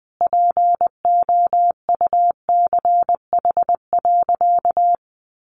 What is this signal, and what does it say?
Morse code, heard as a keyed tone, about 20 wpm, 700 Hz. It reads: POUCH.
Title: Easy Phrases 20wpm